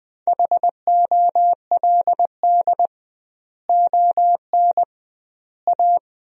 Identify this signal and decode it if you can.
Morse code, heard as a keyed tone, about 20 wpm, 700 Hz. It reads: HOLD ON A
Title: 2x Word-Spacing - Sentences from Top 500 Words - 20wpm